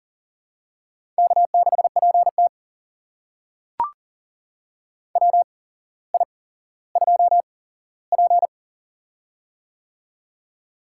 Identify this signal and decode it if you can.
Morse code, heard as a keyed tone, about 42 wpm, spaced as for 17 wpm, 700 Hz. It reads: K6PT WI2P
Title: USA Call Signs - 4 Characters - 8x Character Spacing - 40wpm